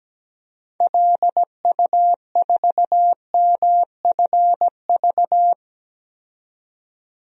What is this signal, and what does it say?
Morse code, heard as a keyed tone, about 17 wpm, 700 Hz. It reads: LU4MFV
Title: DXCC-100 Call Signs - 6 Characters - 17wpm